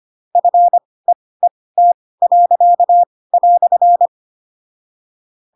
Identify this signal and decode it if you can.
Morse code, heard as a keyed tone, about 25 wpm, spaced as for 18 wpm, 700 Hz. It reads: FEET."
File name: Pinocchio / Pinocchio ch7 12 25 ik1hge